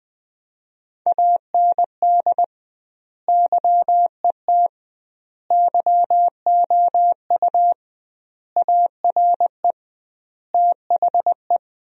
Morse code, heard as a keyed tone, about 21 wpm, 700 Hz. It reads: AND YET YOU ARE THE
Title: 2x Word-Spacing - Sentences from Top 500 Words - 20wpm